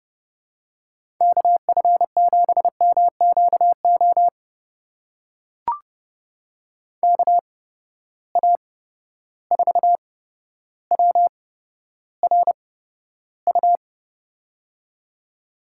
Morse code, heard as a keyed tone, about 32 wpm, spaced as for 13 wpm, 700 Hz. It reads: KF7MQO KA4WRU
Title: USA Call Signs - 6 Characters - 8x Character Spacing - 30wpm